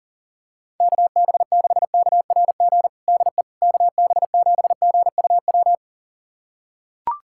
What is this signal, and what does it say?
Morse code, heard as a keyed tone, about 40 wpm, 690 Hz. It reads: KB6KRG DE KB7GUW
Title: USA Callsign DE Callsign - 6 Characters - 40wpm